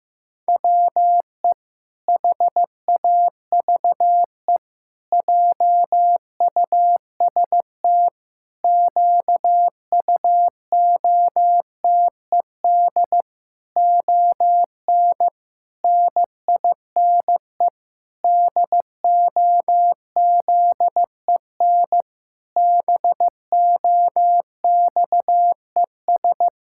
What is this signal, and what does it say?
Morse code, heard as a keyed tone, about 15 wpm, 700 Hz. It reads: WE HAVE JUST QUOTED ON NINE DOZEN BOXES
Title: Pangram Challenge v2 - 15wpm